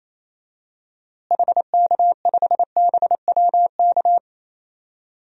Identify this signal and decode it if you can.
Morse code, heard as a keyed tone, about 28 wpm, 700 Hz. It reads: HK5BWK
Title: DXCC-100 Call Signs - 6 Characters - Repeated 3x - 28wpm